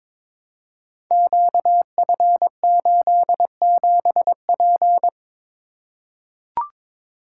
Morse code, heard as a keyed tone, about 22 wpm, 700 Hz. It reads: QF87P
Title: Code Groups - 5 Letter:Numbers - 22wpm